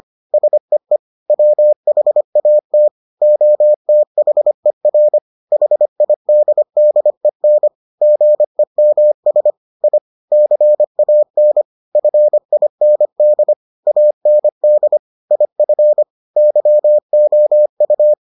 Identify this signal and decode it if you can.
Morse code, heard as a keyed tone, about 25 wpm, 600 Hz. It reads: SEE WHAT OTHER HIDDEN GEMS I CAN FIND AND IF YOU